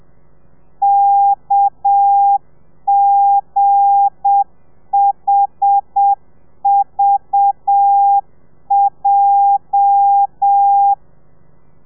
Morse code, heard as a keyed tone, about 7 wpm, 800 Hz. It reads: KGHVJ
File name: telegrafia / i2tfj / Lez8 35